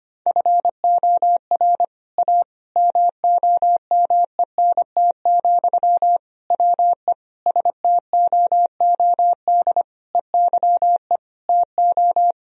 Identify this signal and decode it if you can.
Morse code, heard as a keyed tone, about 25 wpm, 700 Hz. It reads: FOR A MOMENT, WE STOOD EYE TO
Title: Sentences from Top 1000 Words - Review Part 2 - 25wpm